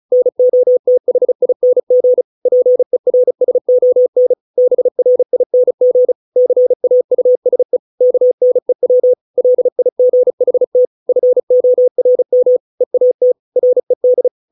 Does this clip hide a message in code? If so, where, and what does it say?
Yes: Morse code from 0.1 to 14.3 s, heard as a keyed tone, about 34 wpm, 500 Hz. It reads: NOTHING PERSON BRING CAUSE KNEW LIGHT FORM EAT RED